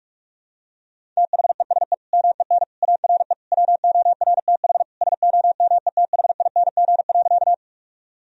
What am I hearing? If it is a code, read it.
Morse code, heard as a keyed tone, about 46 wpm, 700 Hz. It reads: THESE MEN ARE WORTH SOMETHING.
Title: Sentences from Top 1000 Words - Review Part 1 - 45wpm